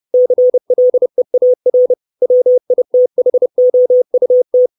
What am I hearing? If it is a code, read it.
Morse code, heard as a keyed tone, about 30 wpm, 500 Hz. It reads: CLEAR WITHOUT
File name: 632 words / 632 Words 30WPM